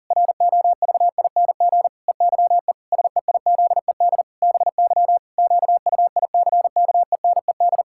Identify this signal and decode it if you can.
Morse code, heard as a keyed tone, about 40 wpm, 700 Hz. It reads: ROVING EYE SEIZED BY QUICKENED
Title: Pangram Challenge v2 - 40wpm